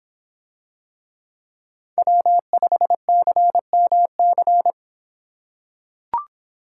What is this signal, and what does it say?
Morse code, heard as a keyed tone, about 26 wpm, 700 Hz. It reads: W5CMC